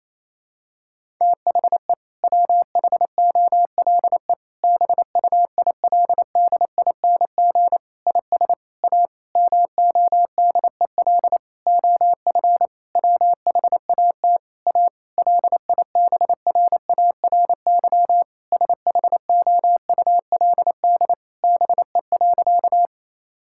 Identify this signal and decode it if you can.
Morse code, heard as a keyed tone, about 27 wpm, 700 Hz. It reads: THE WHOLE BUILDING IS A MODEL OF WHAT A LIBRARY SHOULD BE.
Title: Sentences from Top 1000 Words - Review Part 5 - 28wpm